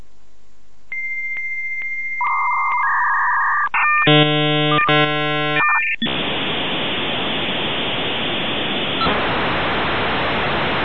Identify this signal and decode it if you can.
Dial-up modem handshake.